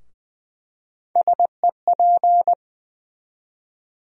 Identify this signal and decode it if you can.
Morse code, heard as a keyed tone, about 20 wpm, 700 Hz. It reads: SEP
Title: USA Call Sign Postfixes - 3 Letters 20wpm